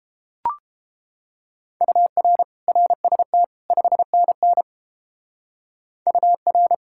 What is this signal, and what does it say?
Morse code, heard as a keyed tone, about 28 wpm, 700 Hz. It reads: UR RST 5NN UR